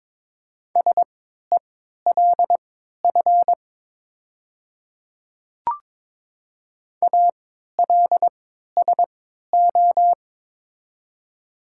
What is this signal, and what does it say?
Morse code, heard as a keyed tone, about 21 wpm, spaced as for 12 wpm, 700 Hz. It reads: SELF ALSO
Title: 3x Character Spacing - Top 200 Words - 22wpm